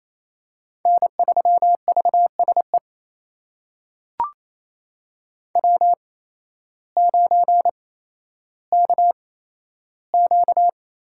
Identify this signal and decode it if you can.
Morse code, heard as a keyed tone, about 29 wpm, spaced as for 13 wpm, 700 Hz. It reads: N3VSE W9KQ
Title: USA Call Signs - 8x Character Spacing - 28wpm